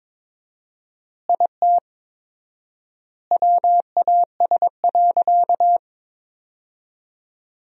Morse code, heard as a keyed tone, about 22 wpm, 700 Hz. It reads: IT WAS.
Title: 4x Word-Spacing - Sentences from Top 100 Words - 22wpm